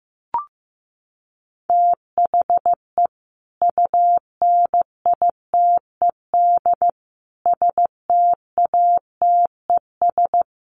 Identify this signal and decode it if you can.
Morse code, heard as a keyed tone, about 15 wpm, 700 Hz. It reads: THE UNITED STATES
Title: Sets of 5 Words 15wpm